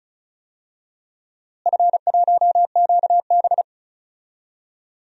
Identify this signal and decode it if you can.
Morse code, heard as a keyed tone, about 36 wpm, 700 Hz. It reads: F1QB
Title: DXCC-100 Call Signs - 4 Characters - Repeated 2x - 35wpm